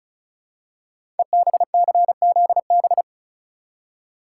Morse code, heard as a keyed tone, about 36 wpm, 690 Hz. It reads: EBCZB
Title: Code Groups - 5 Letters - 35wpm